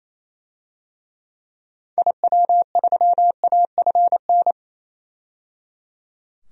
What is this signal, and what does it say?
Morse code, heard as a keyed tone, about 28 wpm, 700 Hz. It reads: IW3AFN
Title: DXCC-100 Call Signs - 6 Characters - 28wpm